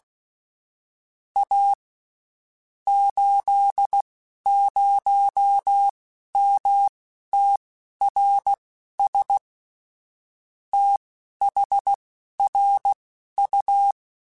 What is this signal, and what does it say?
Morse code, heard as a keyed tone, about 16 wpm, spaced as for 11 wpm, 800 Hz. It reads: A 80MTRS THRU
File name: (spaced) / QSOs Spaced-mp3s / WD0FBT 1 @15 sp